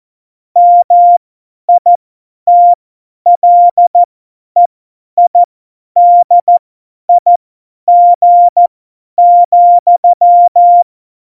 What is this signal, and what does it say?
Morse code, heard as a keyed tone, about 14 wpm, spaced as for 10 wpm, 700 Hz. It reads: MITLEIDIG,
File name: Morsekurs / Witze / Witz 17. September 2019, 50 F70